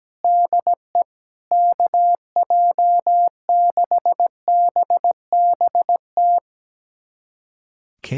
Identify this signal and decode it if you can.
Morse code, heard as a keyed tone, about 17 wpm, 700 Hz. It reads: DE KJ6BBT
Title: USA Callsign DE Callsign - 6 Characters - 17wpm